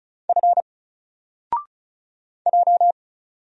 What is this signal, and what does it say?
Morse code, heard as a keyed tone, about 35 wpm, 700 Hz. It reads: F J
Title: Single Character (Letter:Number:Punctuation:Prosign) 35wpm